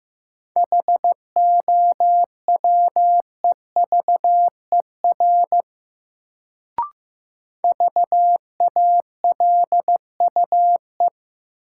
Morse code, heard as a keyed tone, about 15 wpm, 700 Hz. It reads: HOWEVER VALUE
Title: Sentences from Top 600 Words - New Word Drill - 15wpm